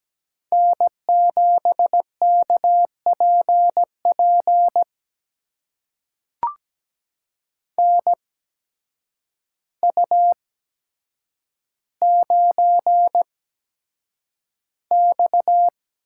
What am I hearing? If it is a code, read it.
Morse code, heard as a keyed tone, about 17 wpm, spaced as for 8 wpm, 700 Hz. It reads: N7KPP NU9X
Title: USA Call Signs - 8x Character Spacing - 17wpm